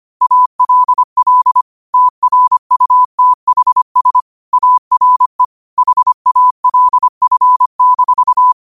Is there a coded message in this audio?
Yes: Morse code from 0.2 to 8.5 s, heard as a keyed tone, about 25 wpm, 1000 Hz. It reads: ALL TRUTHS ARE HALF-